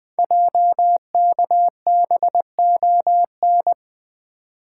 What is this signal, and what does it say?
Morse code, heard as a keyed tone, about 20 wpm, 700 Hz. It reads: JKBON
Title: Code Groups - 5 Letter:Numbers - 20wpm